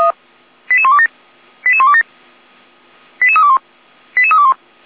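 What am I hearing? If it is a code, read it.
Five-tone selective call (ZVEI), heard as quick runs of five tones, tones of about 70 ms. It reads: 88127 88127 88321 88321
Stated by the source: ZVEI five-tone call 88288 received on SDR.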